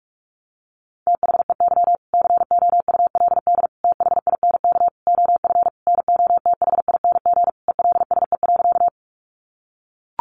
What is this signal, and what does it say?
Morse code, heard as a keyed tone, about 44 wpm, 700 Hz. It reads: THEY COULD THINK OF NOTHING ELSE.
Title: Sentences from Top 800 Words 45wpm